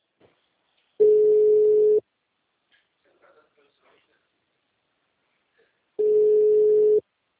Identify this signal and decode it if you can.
Ringback tone (Europe, 425 Hz), with speech.
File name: out- -2038- - - .8